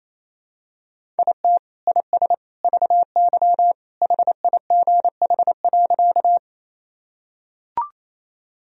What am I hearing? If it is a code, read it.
Morse code, heard as a keyed tone, about 28 wpm, 700 Hz. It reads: IT IS VY HIGH.
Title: Sentences from Top 200 Words and Abbreviations 28wpm